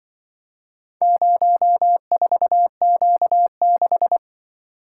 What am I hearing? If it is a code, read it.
Morse code, heard as a keyed tone, about 24 wpm, 700 Hz. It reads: 04Q6
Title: Speed-Racing - Four Letter-Number Combinations - 20wpm